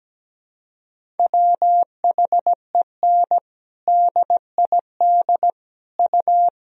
Morse code, heard as a keyed tone, about 17 wpm, 700 Hz. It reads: WHEN DID U
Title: Sentences from Top 300 Words and Abbreviations 17wpm